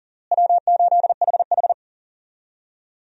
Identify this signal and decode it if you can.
Morse code, heard as a keyed tone, about 37 wpm, 700 Hz. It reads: W8HH
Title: USA Call Signs - 4 Characters - Repeated 3x - 40wpm